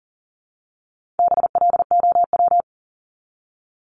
Morse code, heard as a keyed tone, about 40 wpm, 700 Hz. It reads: BLOW
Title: Top 1751-2000 Words 40wpm